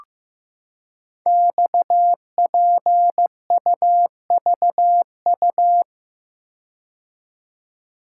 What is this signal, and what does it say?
Morse code, heard as a keyed tone, about 15 wpm, 700 Hz. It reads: XPUVU
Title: Code Groups - 5 Letters - 15wpm